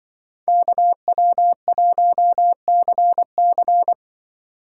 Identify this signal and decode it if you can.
Morse code, heard as a keyed tone, about 24 wpm, 700 Hz. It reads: KW1CC
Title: Speed Racing - USA 2x2 Call Signs 20wpm